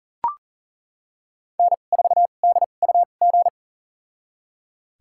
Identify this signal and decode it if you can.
Morse code, heard as a keyed tone, about 37 wpm, spaced as for 28 wpm, 700 Hz. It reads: N4DUG
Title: USA Call Signs - 5 Characters - 2x Character Spacing - 40wpm